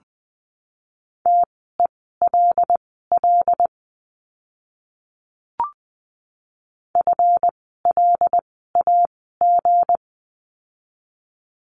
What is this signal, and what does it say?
Morse code, heard as a keyed tone, about 20 wpm, spaced as for 14 wpm, 700 Hz. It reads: TELL FLAG
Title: 160 Lesson 26f - Letter g - Words 20wpm